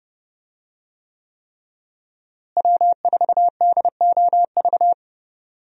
Morse code, heard as a keyed tone, about 28 wpm, 700 Hz. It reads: W4DOV
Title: USA Call Signs - 5 Characters - Repeated 3x - 30wpm